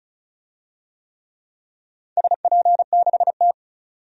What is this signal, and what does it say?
Morse code, heard as a keyed tone, about 36 wpm, 700 Hz. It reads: SP6T
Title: DXCC-100 Call Signs - 4 Characters - 35wpm